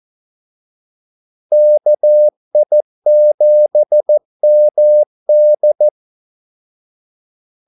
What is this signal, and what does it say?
Morse code, heard as a keyed tone, about 14 wpm, 600 Hz. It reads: KI7MD
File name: Callsigns-14wpm